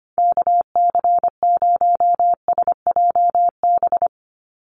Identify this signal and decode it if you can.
Morse code, heard as a keyed tone, about 25 wpm, 700 Hz. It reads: KC0SJB
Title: USA Call Signs 25wpm